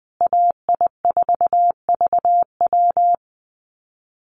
Morse code, heard as a keyed tone, about 20 wpm, 700 Hz. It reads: AI4VW
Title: USA Call Signs Repeated 3x 20wpm